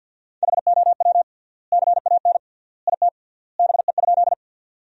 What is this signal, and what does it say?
Morse code, heard as a keyed tone, about 50 wpm, 700 Hz. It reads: HOW CAN IT BE?
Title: 3x Word-Spacing - Sentences from Top 100 Words - 50wpm